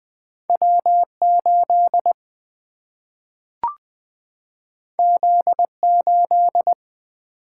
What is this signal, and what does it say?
Morse code, heard as a keyed tone, about 20 wpm, 700 Hz. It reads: W8 Z8
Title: Single-Letter Single-Number 20wpm